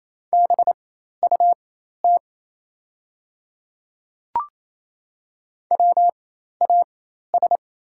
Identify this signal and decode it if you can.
Morse code, heard as a keyed tone, about 28 wpm, spaced as for 13 wpm, 700 Hz. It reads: BUT WAS
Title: 4x Character Spacing - Top 100 Words - 28wpm